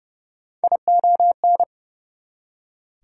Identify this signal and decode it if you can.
Morse code, heard as a keyed tone, about 38 wpm, 700 Hz. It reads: ION